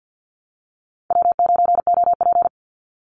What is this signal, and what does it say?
Morse code, heard as a keyed tone, about 50 wpm, 700 Hz. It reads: W9OP
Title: Call Signs - 50wpm